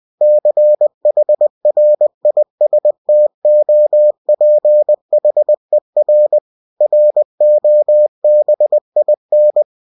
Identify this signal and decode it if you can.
Morse code, heard as a keyed tone, about 20 wpm, 600 Hz. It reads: CHRISTOPHER ROBIN
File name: cw-books / winnie-the-pooh / winnie-20wpm-page-0103